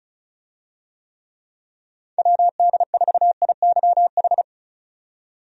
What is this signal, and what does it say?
Morse code, heard as a keyed tone, about 36 wpm, 700 Hz. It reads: WD4IYH
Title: USA Call Signs - 6 Characters - 35wpm